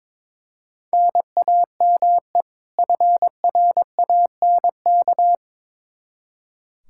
Morse code, heard as a keyed tone, about 21 wpm, 700 Hz. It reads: NAME FRANK
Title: Practice QSO - Exchange 1 - Beginning - 22wpm